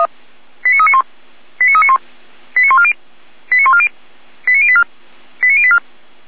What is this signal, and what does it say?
Five-tone selective call (ZVEI), heard as quick runs of five tones, tones of about 70 ms. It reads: 89391 89391 89140 89140 89084 89084
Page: SDR/ZVEI